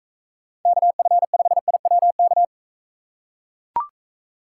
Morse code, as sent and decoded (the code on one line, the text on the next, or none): -.- ..-. ..... .. .-- -.-
KF5IWK